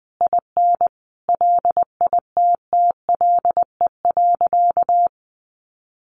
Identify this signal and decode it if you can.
Morse code, heard as a keyed tone, about 20 wpm, 700 Hz. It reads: IN LITTLE.
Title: Sentences from Top 1500 Words 20wpm